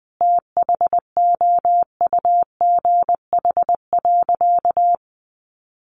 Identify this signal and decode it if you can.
Morse code, heard as a keyed tone, about 19 wpm, 700 Hz. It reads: THOUGH.
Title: Sentences from Top 1400 Words 20wpm